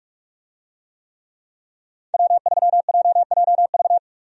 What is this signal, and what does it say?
Morse code, heard as a keyed tone, about 44 wpm, 700 Hz. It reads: W3JJV
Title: USA Call Signs - 5 Characters - 45wpm